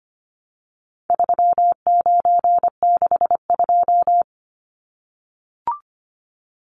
Morse code, heard as a keyed tone, about 25 wpm, 700 Hz. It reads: 3962
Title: Numbers - 4-Digit 25wpm